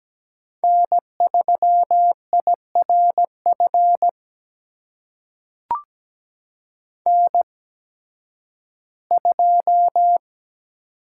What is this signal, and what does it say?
Morse code, heard as a keyed tone, about 17 wpm, spaced as for 11 wpm, 700 Hz. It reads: N3IRF N2
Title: USA Call Signs - 5 Characters - 8x Character Spacing - 17wpm